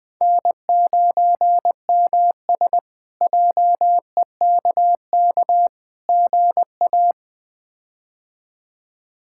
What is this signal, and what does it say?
Morse code, heard as a keyed tone, about 20 wpm, 700 Hz. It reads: N9MS JEKK GA
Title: K1USN SST QSO Practice - 20wpm